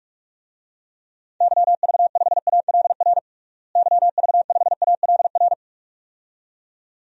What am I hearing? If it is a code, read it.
Morse code, heard as a keyed tone, about 41 wpm, 700 Hz. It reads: YV5ALR YV5ALR
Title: DXCC-100 Call Signs - 6 Characters - Repeated 2x - 45wpm